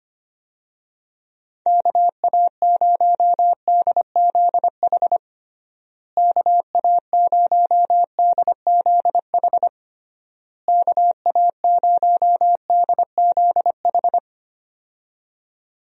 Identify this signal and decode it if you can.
Morse code, heard as a keyed tone, about 25 wpm, 700 Hz. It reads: KA0DZH KA0DZH KA0DZH
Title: USA Call Signs - 6 Characters - Repeated 3x - 25wpm